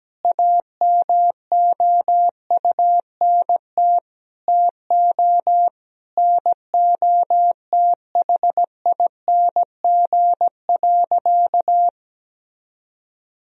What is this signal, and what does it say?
Morse code, heard as a keyed tone, about 17 wpm, 700 Hz. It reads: AMOUNT TO NOTHING.
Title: Sentences from Top 1000 Words - Review Part 4 - 17wpm